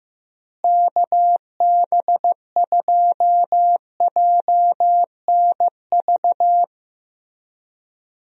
Morse code, heard as a keyed tone, about 15 wpm, 700 Hz. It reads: KB2JNV
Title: USA Call Signs - 6 Characters - Repeated 3x - 15wpm